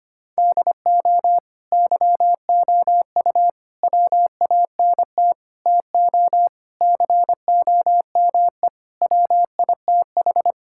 Morse code, heard as a keyed tone, about 25 wpm, 700 Hz. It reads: DO YOU WANT TO COME WITH
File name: practice / 25wpm / Sentences with 3-Word Phrases - 25wpm